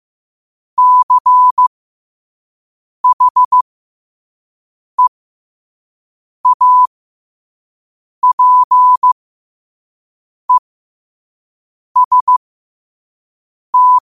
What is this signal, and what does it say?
Morse code, heard as a keyed tone, about 15 wpm, spaced as for 5 wpm, 1000 Hz. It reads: CHEAPEST
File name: qotdfiles / 5wpm